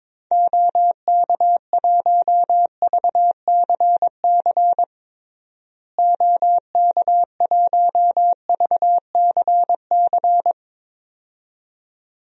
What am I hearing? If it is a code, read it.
Morse code, heard as a keyed tone, about 21 wpm, 700 Hz. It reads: OK1VCC OK1VCC
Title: DXCC-100 Call Signs - 6 Characters - Repeated 2x - 22wpm